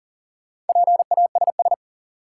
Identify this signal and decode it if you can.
Morse code, heard as a keyed tone, about 40 wpm, 700 Hz. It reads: PASS
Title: Sentences from Top 400 Words - New Word Drill - 40wpm